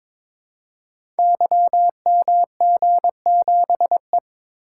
Morse code, heard as a keyed tone, about 22 wpm, 700 Hz. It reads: YMG7E
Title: Code Groups - 5 Letter:Numbers - 22wpm